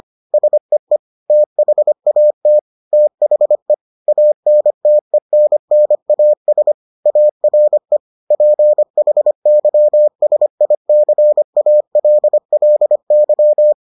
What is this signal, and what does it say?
Morse code, heard as a keyed tone, about 25 wpm, 600 Hz. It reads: SEE THAT THE ANTENNAS ARE PHYSICALLY